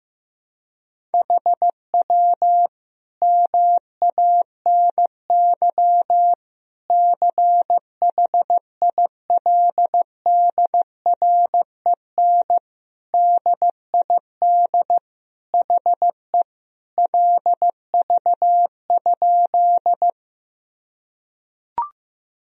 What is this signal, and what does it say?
Morse code, heard as a keyed tone, about 15 wpm, 700 Hz. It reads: HW MANY CHILDREN DID HE LV?
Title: Sentences from Top 400 Words and Abbreviations 15wpm